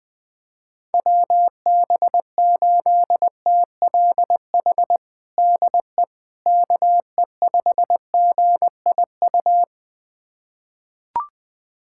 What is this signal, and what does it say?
Morse code, heard as a keyed tone, about 21 wpm, 700 Hz. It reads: WB8TLH DE KE5GIU
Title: USA Callsign DE Callsign - 6 Characters - 20wpm